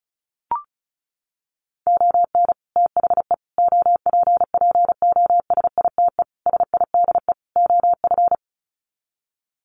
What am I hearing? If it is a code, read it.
Morse code, heard as a keyed tone, about 34 wpm, 700 Hz. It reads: ON THE OPPOSITE SIDE OF
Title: Sets of 5 Words Encore2 35wpm